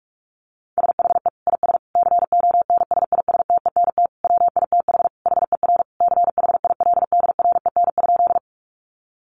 Morse code, heard as a keyed tone, about 44 wpm, 700 Hz. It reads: SHE IS CONSISTENT WITH HER CHILDREN?
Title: Sentences from Top 1500 Words 45wpm